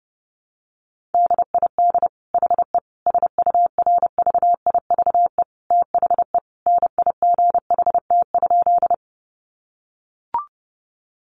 Morse code, heard as a keyed tone, about 30 wpm, 700 Hz. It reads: DID HE SURVIVE THE NIGHT?
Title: Sentences from Top 1300 Words 30wpm